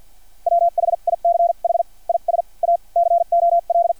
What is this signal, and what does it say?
Morse code, heard as a keyed tone, about 50 wpm, spaced as for 42 wpm, 690 Hz. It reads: WHICH IS A COL
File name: archived / CRSnet